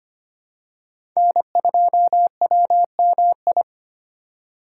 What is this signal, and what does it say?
Morse code, heard as a keyed tone, about 25 wpm, 700 Hz. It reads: N2WMI
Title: USA Call Signs - 5 Characters - 4x Character Spacing - 25wpm